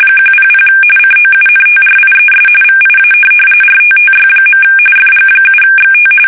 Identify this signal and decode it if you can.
RTTY, about 45.45 baud. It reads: 100 COUNTRIES VIA RTTY. FG7XT
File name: fsk850 / teletype-RTTY